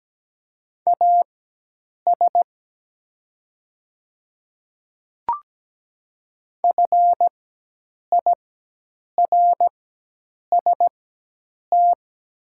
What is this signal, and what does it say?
Morse code, heard as a keyed tone, about 17 wpm, spaced as for 8 wpm, 700 Hz. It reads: AS FIRST
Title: 4x Character Spacing - Top 100 Words - 17wpm